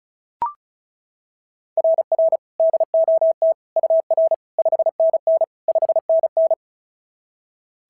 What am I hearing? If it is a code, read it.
Morse code, heard as a keyed tone, about 36 wpm, 690 Hz. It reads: RR DOT UR 5NN 5NN
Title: SOTA QSO Practice 35wpm